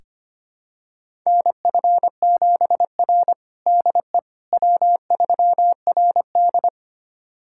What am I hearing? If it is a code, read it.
Morse code, heard as a keyed tone, about 25 wpm, 700 Hz. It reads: NF7R DE W3RD
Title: USA Callsign DE Callsign - 4 Characters - 25wpm